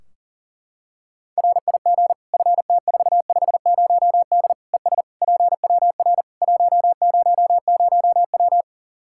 Morse code, heard as a keyed tone, about 37 wpm, 700 Hz. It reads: RIG FT450D ES PWR 100W
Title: Practice QSO - Exchange 2 - Info - 40wpm